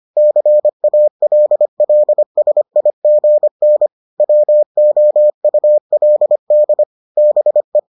Morse code, heard as a keyed tone, about 25 wpm, 600 Hz. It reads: CALLSIGN WOULD BE